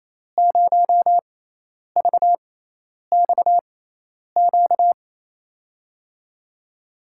Morse code, heard as a keyed tone, about 28 wpm, spaced as for 10 wpm, 700 Hz. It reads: 0VXQ
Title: USA Call Signs - 5 Characters - 6x Character Spacing - 28wpm